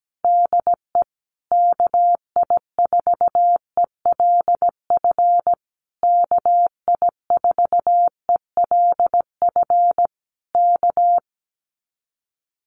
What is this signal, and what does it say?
Morse code, heard as a keyed tone, about 17 wpm, 700 Hz. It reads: DE KI4ELF KI4ELF K
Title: Common QSO Phrases Encore 17wpm